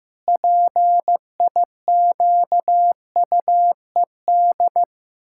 Morse code, heard as a keyed tone, about 15 wpm, 700 Hz. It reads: PIQUED
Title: Pangram Challenge v2 - 15wpm